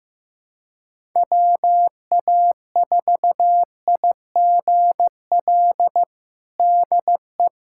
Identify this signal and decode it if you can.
Morse code, heard as a keyed tone, about 14 wpm, 700 Hz. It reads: WA4IGL DE
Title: USA Callsign DE Callsign - 6 Characters - 15wpm